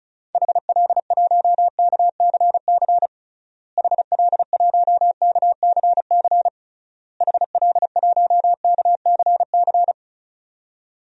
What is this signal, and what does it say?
Morse code, heard as a keyed tone, about 36 wpm, 700 Hz. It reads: HL1KCC HL1KCC HL1KCC
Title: DXCC-100 Call Signs - 6 Characters - Repeated 3x - 35wpm